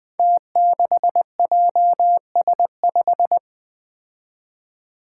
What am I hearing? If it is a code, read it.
Morse code, heard as a keyed tone, about 20 wpm, 700 Hz. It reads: T6JS5
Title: Five Challenging Letter-Number Combinations 20wpm